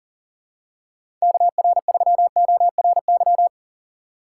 Morse code, heard as a keyed tone, about 40 wpm, 700 Hz. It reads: KR3ORY
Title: USA Call Signs - 6 Characters - 6x Character Spacing - 40wpm